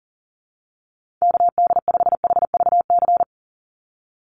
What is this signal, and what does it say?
Morse code, heard as a keyed tone, about 38 wpm, 700 Hz. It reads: KD5HVC
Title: USA Call Signs 40wpm